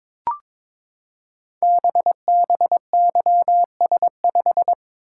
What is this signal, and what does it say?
Morse code, heard as a keyed tone, about 22 wpm, 700 Hz. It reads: BBYS5
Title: Code Groups - 5 Letter:Numbers - 22wpm